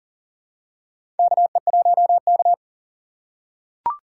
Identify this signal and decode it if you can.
Morse code, heard as a keyed tone, about 40 wpm, 700 Hz. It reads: KE1K
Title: USA Call Signs - 3x Character Spacing - 40wpm